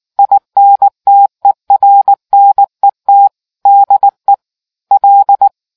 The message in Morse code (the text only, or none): INTERNET DE L